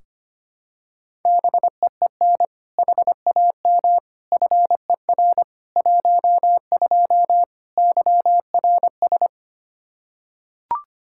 Morse code, heard as a keyed tone, about 25 wpm, 700 Hz. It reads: BEEN HAM FER 12 YRS
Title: Practice QSO - Exchange 3 - More Info - 25wpm